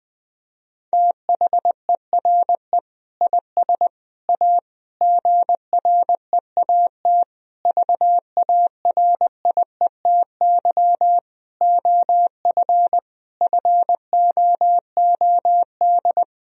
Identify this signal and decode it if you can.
Morse code, heard as a keyed tone, about 20 wpm, 700 Hz. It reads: THERE IS A GREAT VARIETY OF FOOD
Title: Sentences from Top 1000 Words - Review Part 4 - 20wpm